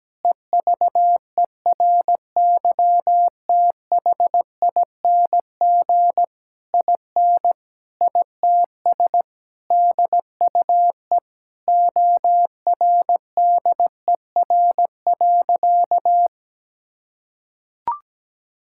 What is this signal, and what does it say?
Morse code, heard as a keyed tone, about 17 wpm, 700 Hz. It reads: EVERYTHING IN ITS DUE ORDER.
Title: Sentences from Top 1000 Words - Review Part 6 - 17wpm